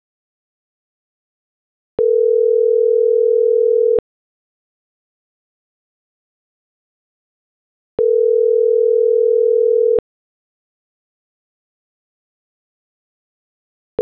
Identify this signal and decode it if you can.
Ringback tone (North America), 440 + 480 Hz.